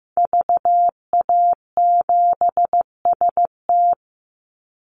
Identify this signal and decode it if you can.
Morse code, heard as a keyed tone, about 15 wpm, 700 Hz. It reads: VA7ST
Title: Call Signs Repeated 3x - 15wpm